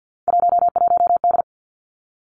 Morse code, heard as a keyed tone, about 50 wpm, 700 Hz. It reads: 11D
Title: Three Challenging Letter-Number Combinations 50wpm